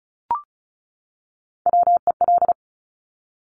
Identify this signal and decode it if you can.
Morse code, heard as a keyed tone, about 36 wpm, 700 Hz. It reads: WEL
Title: Top Three Letter Combinations in Words 35wpm